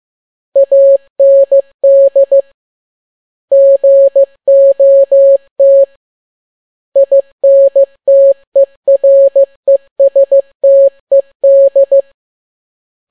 Morse code, heard as a keyed tone, about 15 wpm, 550 Hz. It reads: AND GOT INTERESTED
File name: n6tt p1 15C13